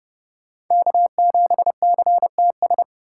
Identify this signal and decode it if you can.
Morse code, heard as a keyed tone, about 31 wpm, 700 Hz. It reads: K7CTS
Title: USA Call Signs - 5 Characters - 4x Character Spacing - 30wpm